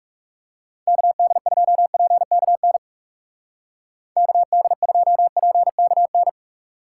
Morse code, heard as a keyed tone, about 38 wpm, 700 Hz. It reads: KD2PKN KD2PKN